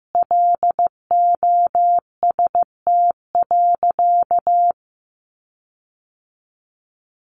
Morse code, heard as a keyed tone, about 15 wpm, 700 Hz. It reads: LOST.
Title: Sentences from Top 1400 Words 15wpm